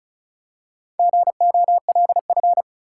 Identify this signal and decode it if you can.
Morse code, heard as a keyed tone, about 34 wpm, 700 Hz. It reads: GOLF
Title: Four Letter Words - 35wpm